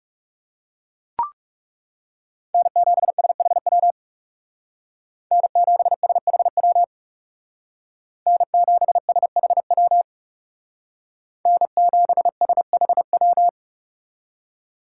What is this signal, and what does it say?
Morse code, heard as a keyed tone, about 36 wpm, 700 Hz. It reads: N7SHW N7SHW N7SHW N7SHW
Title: Speed Racing - USA 1x3 Call Signs 30wpm